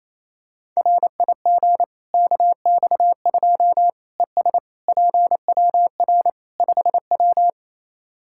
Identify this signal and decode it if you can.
Morse code, heard as a keyed tone, about 27 wpm, 700 Hz. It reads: RIG KX2 ES PWR 5W
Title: K1USN SST QSO Practice 28wpm